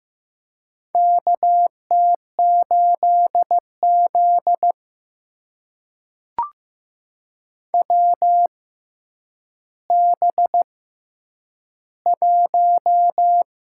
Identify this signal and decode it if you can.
Morse code, heard as a keyed tone, about 15 wpm, spaced as for 9 wpm, 700 Hz. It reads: KT8Z WB1
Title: USA Call Signs - 4 Characters - 6x Character Spacing - 15wpm